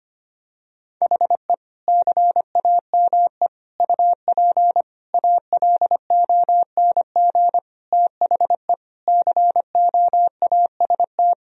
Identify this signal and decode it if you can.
Morse code, heard as a keyed tone, about 25 wpm, 700 Hz. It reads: HE CAME UP ALONG THE COAST
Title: Sentences from Top 1000 Words - Review Part 4 - 25wpm